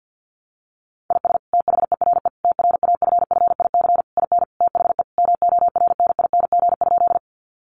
Morse code, heard as a keyed tone, about 48 wpm, 700 Hz. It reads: IS THERE TRAFFIC IN THE MORNING?
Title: Sentences from Top 1300 Words 50wpm